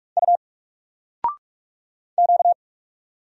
Morse code, heard as a keyed tone, about 46 wpm, 700 Hz. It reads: U Q
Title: Single Letter-Number 45wpm